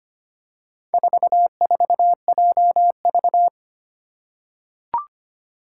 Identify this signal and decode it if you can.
Morse code, heard as a keyed tone, about 25 wpm, 700 Hz. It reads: 44JV
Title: Four Challenging Letter-Number Combinations 25wpm